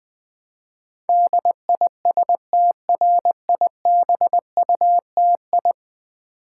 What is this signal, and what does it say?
Morse code, heard as a keyed tone, about 20 wpm, 700 Hz. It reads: DISTRIBUTI
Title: Build A Word - 5 Letters At A Time - 12-Letter Words - 20wpm